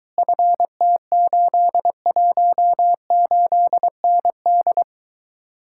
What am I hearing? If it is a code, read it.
Morse code, heard as a keyed tone, about 23 wpm, 700 Hz. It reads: FT818ND